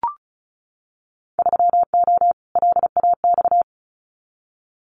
Morse code, heard as a keyed tone, about 36 wpm, 700 Hz. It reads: 3O LAX
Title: Winter Field Day Exchanges Encore 35wpm